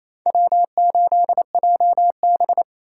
Morse code, heard as a keyed tone, about 28 wpm, 700 Hz. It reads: W8JB
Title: USA Call Signs - 4 Characters - 6x Character Spacing - 28wpm